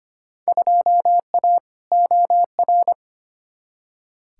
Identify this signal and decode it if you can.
Morse code, heard as a keyed tone, about 25 wpm, 700 Hz. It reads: 2A OR
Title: Field Day Exchanges - 25wpm